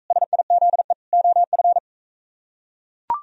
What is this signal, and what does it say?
Morse code, heard as a keyed tone, about 42 wpm, 700 Hz. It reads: SIZE OF